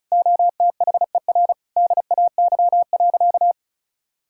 Morse code, heard as a keyed tone, about 34 wpm, 700 Hz. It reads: OTHER DAY.
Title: Sentences from Top 1000 Words - Review Part 5 - 35wpm